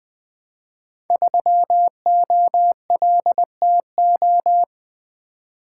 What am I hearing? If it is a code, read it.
Morse code, heard as a keyed tone, about 20 wpm, 700 Hz. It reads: 3OLTO
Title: Code Groups - 5 Letter:Numbers - 20wpm